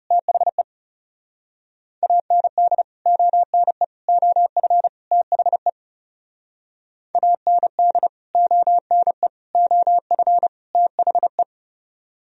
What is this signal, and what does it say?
Morse code, heard as a keyed tone, about 40 wpm, 700 Hz. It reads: THE AND ONE OF THE AND ONE OF THE